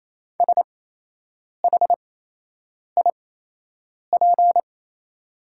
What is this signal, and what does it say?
Morse code, heard as a keyed tone, about 28 wpm, spaced as for 8 wpm, 700 Hz. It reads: SHIP
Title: 8x Character Spacing - Top 500 Words - 28wpm